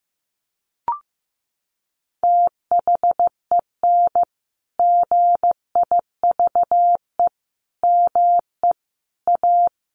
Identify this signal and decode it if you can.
Morse code, heard as a keyed tone, about 15 wpm, 700 Hz. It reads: THEN GIVE ME A
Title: Sentences from Top 1300 Words 15wpm